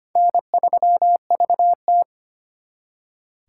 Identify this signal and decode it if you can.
Morse code, heard as a keyed tone, about 32 wpm, 700 Hz. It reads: N3VT